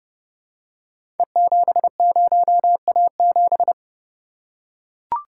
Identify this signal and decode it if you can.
Morse code, heard as a keyed tone, about 28 wpm, 700 Hz. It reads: E70A7
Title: Code Groups - 5 Letter:Numbers - 30wpm